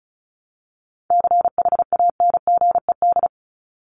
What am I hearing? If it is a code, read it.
Morse code, heard as a keyed tone, about 34 wpm, 700 Hz. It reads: CHANGED
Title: ED Suffix - Top 100 Words with Base Words 35wpm